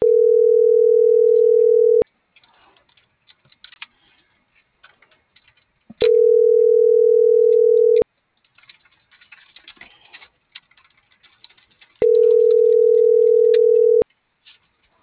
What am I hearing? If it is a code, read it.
Ringback tone (North America), 440 + 480 Hz.